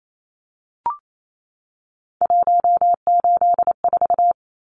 Morse code, heard as a keyed tone, about 28 wpm, 700 Hz. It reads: 184
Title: Numbers - 3-Digit 28wpm